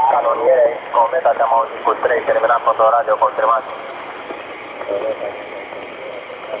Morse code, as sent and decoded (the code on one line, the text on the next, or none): none